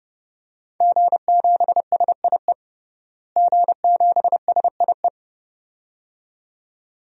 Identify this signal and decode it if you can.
Morse code, heard as a keyed tone, about 31 wpm, 700 Hz. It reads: G7SIE G7SIE
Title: DXCC-100 Call Signs - 5 Characters - Repeated 2x - 30wpm